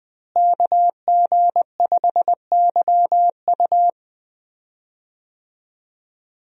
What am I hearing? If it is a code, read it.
Morse code, heard as a keyed tone, about 21 wpm, 700 Hz. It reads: KG5YU
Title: USA Call Signs - 5 Characters - Repeated 2x - 20wpm